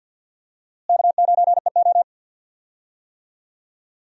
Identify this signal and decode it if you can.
Morse code, heard as a keyed tone, about 52 wpm, 700 Hz. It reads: K9EO
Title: USA Call Signs - 4 Characters - Repeated 3x - 50wpm